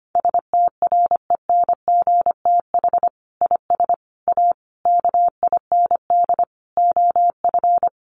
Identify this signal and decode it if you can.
Morse code, heard as a keyed tone, about 25 wpm, 700 Hz. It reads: STRENGTH IS A KIND OF
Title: Sentences from Top 1300 Words 25wpm